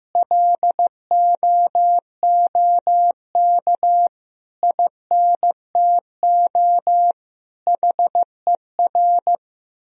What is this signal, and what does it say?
Morse code, heard as a keyed tone, about 15 wpm, 700 Hz. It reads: LOOK INTO HER
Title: Sentences from Top 1000 Words - Review Part 4 - 15wpm